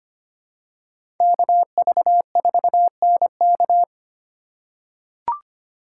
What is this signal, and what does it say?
Morse code, heard as a keyed tone, about 25 wpm, 700 Hz. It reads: KV4NK
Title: USA Call Signs - 3x Character Spacing - 25wpm